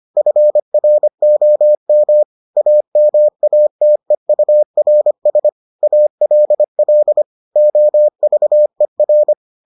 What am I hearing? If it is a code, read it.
Morse code, heard as a keyed tone, about 25 wpm, 600 Hz. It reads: FROM AMATEURS ALL OVER